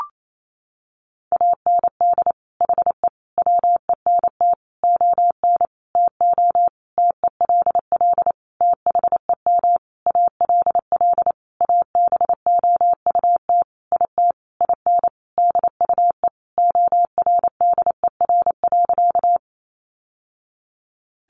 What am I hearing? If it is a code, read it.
Morse code, heard as a keyed tone, about 28 wpm, 700 Hz. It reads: AND HE WENT ON TO TELL THEM ALL ABOUT IT IN DUE ORDER.
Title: Sentences from Top 900 Words 28wpm